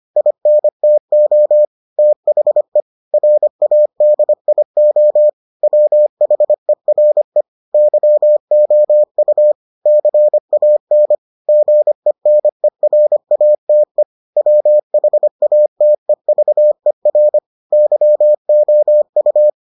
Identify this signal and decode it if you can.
Morse code, heard as a keyed tone, about 25 wpm, 600 Hz. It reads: INTO THE RADIO WHERE YOU CAN GENERATE WHATEVER YOU